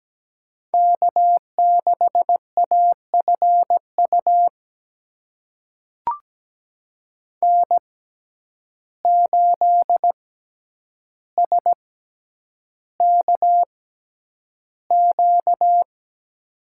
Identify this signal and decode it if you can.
Morse code, heard as a keyed tone, about 17 wpm, spaced as for 9 wpm, 700 Hz. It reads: K6AFU N8SKQ
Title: USA Call Signs - 5 Characters - 6x Character Spacing - 17wpm